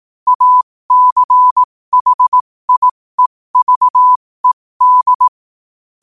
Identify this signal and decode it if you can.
Morse code, heard as a keyed tone, about 18 wpm, spaced as for 15 wpm, 1000 Hz. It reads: ACHIEVED
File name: qotdfiles / 15wpm